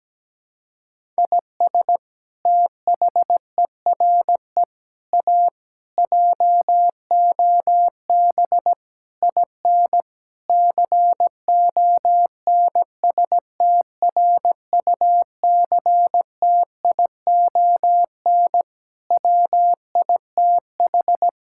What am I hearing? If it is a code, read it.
Morse code, heard as a keyed tone, about 17 wpm, 700 Hz. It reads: IS THERE A JOB IN CONSTRUCTION WITH
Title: Sentences from Top 1600 Words 17wpm